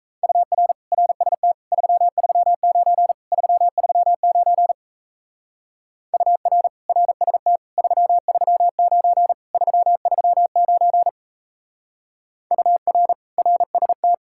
Morse code, heard as a keyed tone, about 37 wpm, 700 Hz. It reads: UR RST 339 339 UR RST 339 339 UR RST